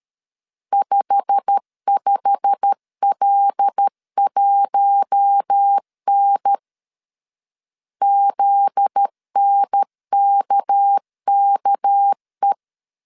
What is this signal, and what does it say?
Morse code, heard as a keyed tone, about 13 wpm, 800 Hz. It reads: 55L1N ZNKKE
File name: CW / DET / 1 9 1500C S A 12wpm 3800